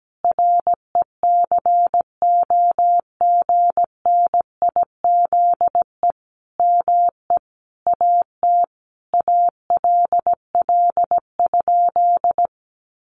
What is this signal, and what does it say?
Morse code, heard as a keyed tone, about 17 wpm, 700 Hz. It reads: RECOGNIZE ME AT ALL?
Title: Sentences from Top 1400 Words 17wpm